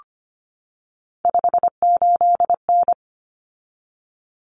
Morse code, heard as a keyed tone, about 25 wpm, 700 Hz. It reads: 58N
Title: Common Cut Numbers - 3-Digit 25wpm